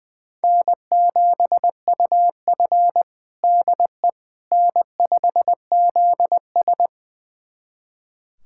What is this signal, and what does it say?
Morse code, heard as a keyed tone, about 20 wpm, 700 Hz. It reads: N7UF DE N5ZS
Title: USA Callsign DE Callsign - 4 Characters - 20wpm